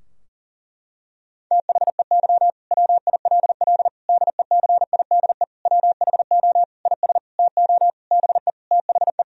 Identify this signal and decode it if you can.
Morse code, heard as a keyed tone, about 40 wpm, 700 Hz. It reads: THEY WILL DECIDE WHO IS TO BE THE
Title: Sentences from Top 1000 Words - Review Part 3 - 40wpm